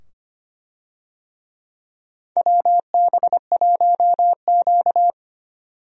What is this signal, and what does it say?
Morse code, heard as a keyed tone, about 25 wpm, 700 Hz. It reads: WB1Q
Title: USA Call Signs - 4 Characters - 25wpm